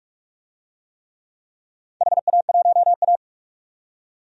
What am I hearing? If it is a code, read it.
Morse code, heard as a keyed tone, about 47 wpm, 700 Hz. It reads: HA1A
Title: DXCC-100 Call Signs - 4 Characters - 45wpm